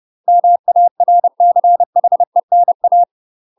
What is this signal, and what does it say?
Morse code, heard as a keyed tone, about 30 wpm, 700 Hz. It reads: MARCHENA
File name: UK MASTER QSOs TRAINING / 30WPM / UK MASTER QSOs 0016